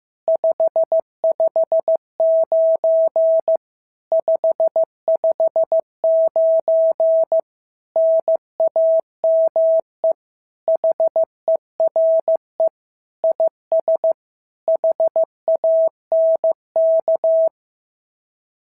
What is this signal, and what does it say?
Morse code, heard as a keyed tone, about 14 wpm, 650 Hz. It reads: 559 559 NAME HERE IS HANK
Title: SOTA QSO Practice 15wpm